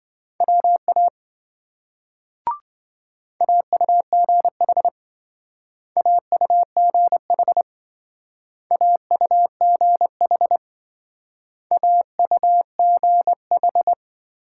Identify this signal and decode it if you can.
Morse code, heard as a keyed tone, about 25 wpm, 700 Hz. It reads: WA AUGH AUGH AUGH AUGH
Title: Speed-Racing Multi-letter Phonemes - 20wpm